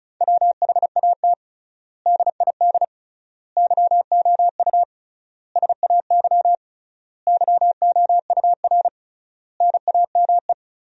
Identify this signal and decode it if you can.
Morse code, heard as a keyed tone, about 36 wpm, 700 Hz. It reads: WHAT DID YOU SAY YOUR NAME
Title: 3x Word-Spacing - Sentences from Top 200 Words - 35wpm